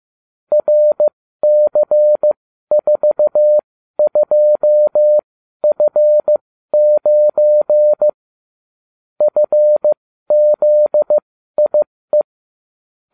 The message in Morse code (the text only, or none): RC42F9 FZIE